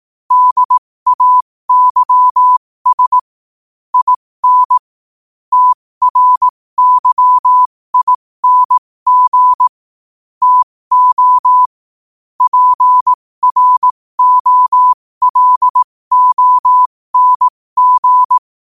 Morse code, heard as a keyed tone, about 18 wpm, spaced as for 15 wpm, 1000 Hz. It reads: DAYS IN TRYING TO PROLONG
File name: qotdfiles / 15wpm